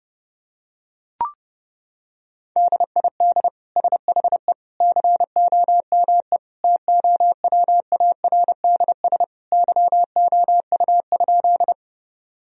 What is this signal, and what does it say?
Morse code, heard as a keyed tone, about 32 wpm, 700 Hz. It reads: DID SHE COME TOWARDS YOU?
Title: Sentences from Top 1000 Words - Review Part 2 - 30wpm